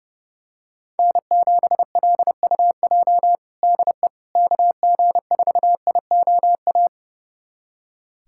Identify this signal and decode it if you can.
Morse code, heard as a keyed tone, about 30 wpm, 700 Hz. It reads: N7LUJ DE KG4IOA
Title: USA Callsign DE Callsign 30wpm